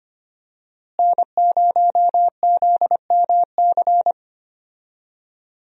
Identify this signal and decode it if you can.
Morse code, heard as a keyed tone, about 25 wpm, 700 Hz. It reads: N0ZMC
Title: USA Call Signs - 5 Characters - Repeated 3x - 25wpm